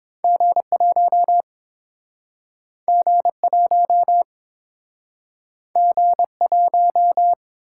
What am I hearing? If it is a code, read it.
Morse code, heard as a keyed tone, about 32 wpm, 700 Hz. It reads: G1 G1 G1